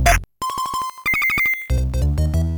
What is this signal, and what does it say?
There is no code.